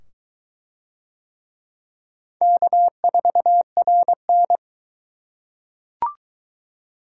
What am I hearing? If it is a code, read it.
Morse code, heard as a keyed tone, about 22 wpm, 700 Hz. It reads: K4RN